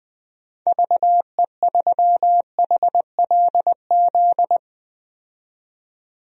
Morse code, heard as a keyed tone, about 20 wpm, 700 Hz. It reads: VE3HLZ
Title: DXCC-100 Call Signs - 6 Characters - 20wpm